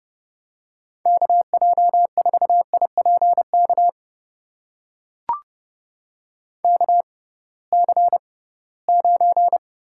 Morse code, heard as a keyed tone, about 30 wpm, spaced as for 19 wpm, 700 Hz. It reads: KJ4IPK KC9
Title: USA Call Signs - 6 Characters - 6x Character Spacing - 30wpm